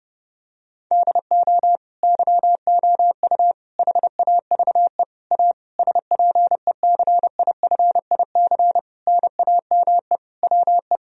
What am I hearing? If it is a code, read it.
Morse code, heard as a keyed tone, about 30 wpm, 700 Hz. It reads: DO YOU HAVE A SPECIFIC NAME WE
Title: Sentences from Top 1000 Words - Review Part 3 - 30wpm